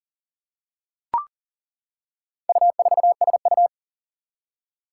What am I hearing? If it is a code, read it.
Morse code, heard as a keyed tone, about 40 wpm, 700 Hz. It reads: U4SU
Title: Four Challenging Letter-Number Combinations 40wpm